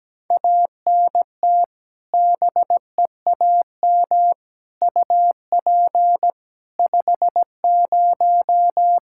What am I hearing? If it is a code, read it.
Morse code, heard as a keyed tone, about 17 wpm, 700 Hz. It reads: ANT BEAM UP 50
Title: Practice QSO - Exchange 2 - Info - 17wpm